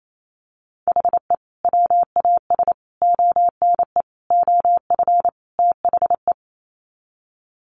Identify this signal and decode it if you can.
Morse code, heard as a keyed tone, about 28 wpm, 700 Hz. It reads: HE WAS ONE OF THE
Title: Sets of 5 Words 28wpm